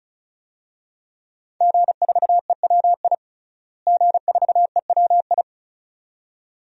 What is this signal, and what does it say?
Morse code, heard as a keyed tone, about 34 wpm, 700 Hz. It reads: G4EWI G4EWI
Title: DXCC-100 Call Signs - 5 Characters - Repeated 2x - 35wpm